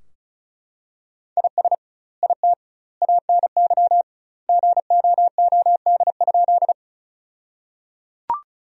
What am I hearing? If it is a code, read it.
Morse code, heard as a keyed tone, about 36 wpm, 700 Hz. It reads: IS IT ANY GOOD?
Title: 2x Word-Spacing - Sentences from Top 200 Words - 35wpm